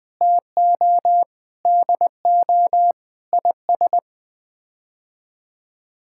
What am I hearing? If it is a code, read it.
Morse code, heard as a keyed tone, about 21 wpm, 700 Hz. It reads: TO DO IS
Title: Sets of 8 Words - 20wpm